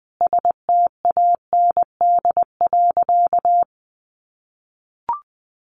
Morse code, heard as a keyed tone, about 20 wpm, 700 Hz. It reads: STAND.
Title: Additional Easy Phrases 20wpm